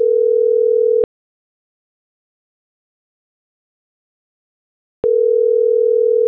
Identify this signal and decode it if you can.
Ringback tone (North America), 440 + 480 Hz.